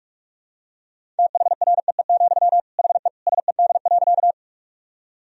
Morse code, heard as a keyed tone, about 46 wpm, 700 Hz. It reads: THREE, HE SED.
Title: Sentences from Top 200 Words and Abbreviations 45wpm